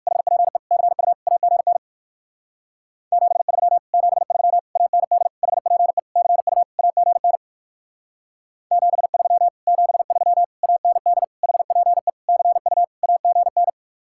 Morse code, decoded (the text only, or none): HPE CU AGN 73 73 AND HPE CU AGN 73 73 AND HPE CU AGN